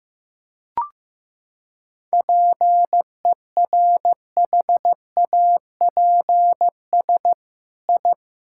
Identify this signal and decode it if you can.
Morse code, heard as a keyed tone, about 16 wpm, 700 Hz. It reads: PERHAPS I
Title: Sentences from Top 1000 Words - Review Part 5 - 15wpm